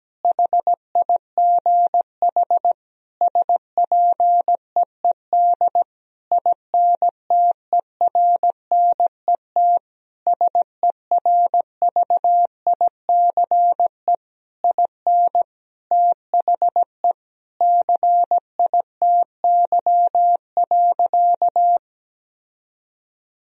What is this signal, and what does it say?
Morse code, heard as a keyed tone, about 17 wpm, 700 Hz. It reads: HIGH SPEED INTERNET SERVICE IN THE CITY.
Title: Sentences from Top 1000 Words - Review Part 4 - 17wpm